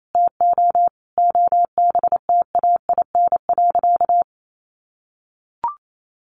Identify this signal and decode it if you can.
Morse code, heard as a keyed tone, about 28 wpm, 700 Hz. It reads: TO OBTAIN.
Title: Sentences from Top 1300 Words 28wpm